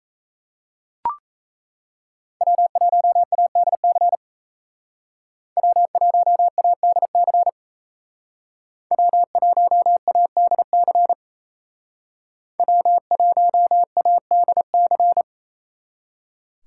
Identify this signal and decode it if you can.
Morse code, heard as a keyed tone, about 36 wpm, 700 Hz. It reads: W1ADC W1ADC W1ADC W1ADC